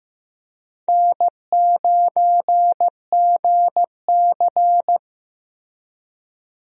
Morse code, heard as a keyed tone, about 15 wpm, 700 Hz. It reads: N9GC
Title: USA Call Signs - 4 Characters - Repeated 3x - 15wpm